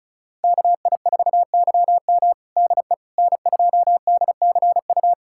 Morse code, heard as a keyed tone, about 36 wpm, 700 Hz. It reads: KI4YM DE N2DCU
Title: USA Callsign DE Callsign - 5 Characters - 35wpm